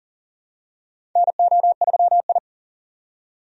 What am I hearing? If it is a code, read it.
Morse code, heard as a keyed tone, about 40 wpm, 700 Hz. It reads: NO3I
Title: USA Call Signs - 4 Characters - 40wpm